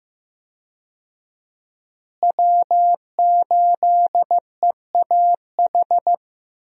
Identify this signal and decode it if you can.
Morse code, heard as a keyed tone, about 15 wpm, 700 Hz. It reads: W8EAH
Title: USA Call Signs - 5 Characters - Repeated 2x - 15wpm